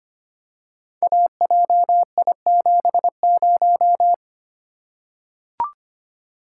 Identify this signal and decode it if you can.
Morse code, heard as a keyed tone, about 25 wpm, 700 Hz. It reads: AJI70
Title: Code Groups - 5 Letter:Numbers - 25wpm